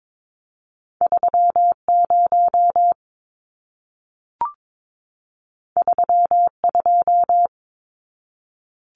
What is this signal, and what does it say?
Morse code, heard as a keyed tone, about 22 wpm, 700 Hz. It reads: 30 32
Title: Numbers - 2-Digit 22wpm